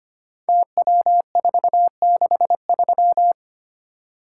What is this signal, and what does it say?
Morse code, heard as a keyed tone, about 25 wpm, 700 Hz. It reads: TW463
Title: Code Groups - 5 Letter:Numbers - 25wpm